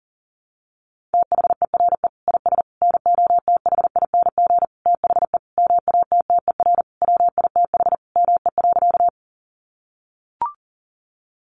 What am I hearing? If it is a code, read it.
Morse code, heard as a keyed tone, about 40 wpm, 700 Hz. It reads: THERE IS NOTHING THE MATTER WITH ME.
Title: Sentences from Top 700 Words 40wpm